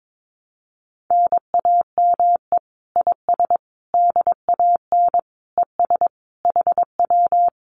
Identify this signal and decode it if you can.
Morse code, heard as a keyed tone, about 22 wpm, 700 Hz. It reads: NAME IS DAN ES HW
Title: Common QSO Phrases Encore 22wpm